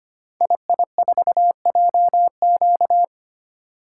Morse code, heard as a keyed tone, about 25 wpm, 700 Hz. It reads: II4JQ
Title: Code Groups - 5 Letter:Numbers - 25wpm